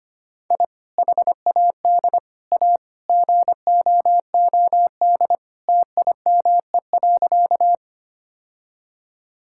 Morse code, heard as a keyed tone, about 25 wpm, 700 Hz. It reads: I HAD A GOOD TIME.
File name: practice / 25wpm / Sentences with 3-Word Phrases - 25wpm